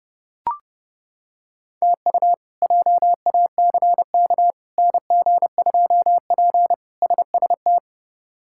Morse code, heard as a keyed tone, about 30 wpm, 700 Hz. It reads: TU JACK NG2P SST
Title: K1USN SST QSO Practice - 30wpm